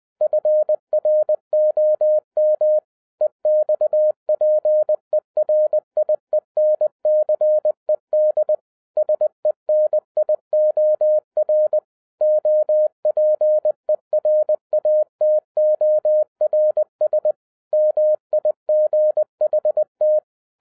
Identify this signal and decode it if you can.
Morse code, heard as a keyed tone, about 20 wpm, 610 Hz. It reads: FROM EXPERIENCED SENIOR OPERATORS MIGHT